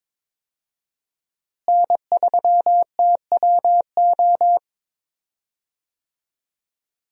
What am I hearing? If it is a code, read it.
Morse code, heard as a keyed tone, about 21 wpm, 700 Hz. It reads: N3TWO
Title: USA Call Signs - 5 Characters - Repeated 2x - 22wpm